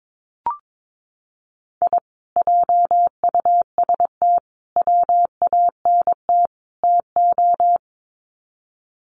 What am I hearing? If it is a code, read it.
Morse code, heard as a keyed tone, about 22 wpm, 700 Hz. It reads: I JUST WANT TO
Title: Sets of 4 Words 22wpm